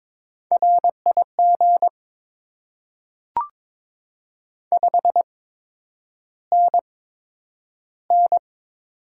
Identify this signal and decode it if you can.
Morse code, heard as a keyed tone, about 22 wpm, spaced as for 9 wpm, 700 Hz. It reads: RIG 5NN
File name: Top 20 QSO Elements - 8x Character Spacing - 22wpm